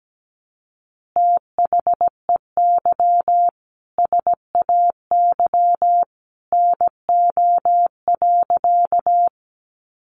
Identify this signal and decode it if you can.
Morse code, heard as a keyed tone, about 17 wpm, 700 Hz. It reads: THEY SAY NO.
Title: Easy Phrases 17wpm